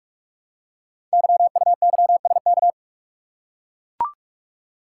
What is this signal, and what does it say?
Morse code, heard as a keyed tone, about 44 wpm, 700 Hz. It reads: YUYSK
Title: Code Groups - 5 Letters - 45wpm